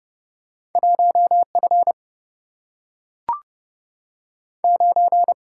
Morse code, heard as a keyed tone, about 30 wpm, 700 Hz. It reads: 1 F 9
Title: Single-Letter Single-Number 30wpm